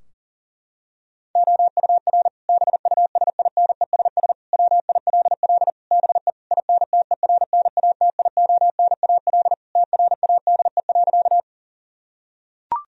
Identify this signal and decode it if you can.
Morse code, heard as a keyed tone, about 40 wpm, 700 Hz. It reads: OUR BUSINESS WILL BE INTERNATIONAL TRADE.
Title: Sentences from Top 1000 Words - Review Part 4 - 40wpm